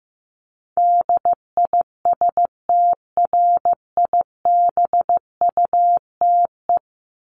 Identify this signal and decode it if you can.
Morse code, heard as a keyed tone, about 15 wpm, 700 Hz. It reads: DISTRIBUTE
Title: Top 1501-1750 Words 15wpm